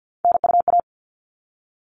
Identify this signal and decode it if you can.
Morse code, heard as a keyed tone, about 50 wpm, 700 Hz. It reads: NUA
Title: All Cut Numbers - 3-Digit 50wpm